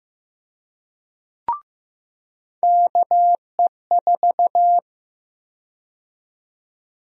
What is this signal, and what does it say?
Morse code, heard as a keyed tone, about 15 wpm, 700 Hz. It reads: KE4
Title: USA Call Sign Prefixes - 15wpm